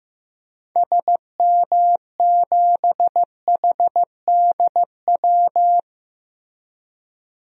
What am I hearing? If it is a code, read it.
Morse code, heard as a keyed tone, about 16 wpm, 700 Hz. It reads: SM7HDW
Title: DXCC-100 Call Signs - 6 Characters - Repeated 2x - 15wpm